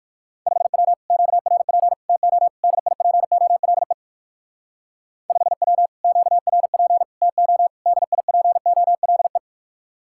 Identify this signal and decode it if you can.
Morse code, heard as a keyed tone, about 59 wpm, 700 Hz. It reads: 5W QRP TO DIPOLE 5W QRP TO DIPOLE